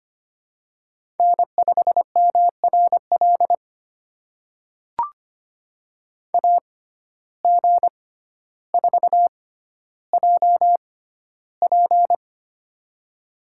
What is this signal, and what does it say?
Morse code, heard as a keyed tone, about 25 wpm, spaced as for 13 wpm, 700 Hz. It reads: N5MRL AG4JP
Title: USA Call Signs - 5 Characters - 6x Character Spacing - 25wpm